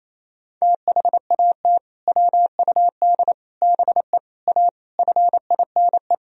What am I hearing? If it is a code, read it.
Morse code, heard as a keyed tone, about 28 wpm, 700 Hz. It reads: THAT WUD BE A FINE
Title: Sentences from Top 500 Words and Abbreviations 28wpm